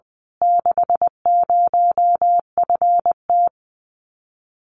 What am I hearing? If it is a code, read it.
Morse code, heard as a keyed tone, about 19 wpm, 700 Hz. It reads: 60FT
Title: Common QSO Phrases Encore 20wpm